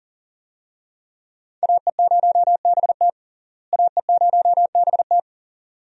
Morse code, heard as a keyed tone, about 40 wpm, 700 Hz. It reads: AE0BT AE0BT
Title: USA Call Signs - 5 Characters - Repeated 2x - 40wpm